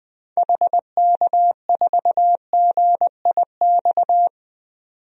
Morse code, heard as a keyed tone, about 20 wpm, 700 Hz. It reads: HK4GIX
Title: DXCC-100 Call Signs - 6 Characters - 20wpm